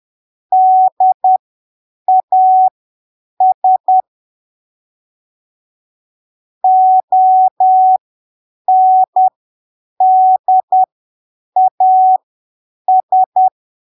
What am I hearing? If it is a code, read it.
Morse code, heard as a keyed tone, about 10 wpm, spaced as for 7 wpm, 750 Hz. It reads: DAS ONDAS
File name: arquivos / texto / texto 10 10ppm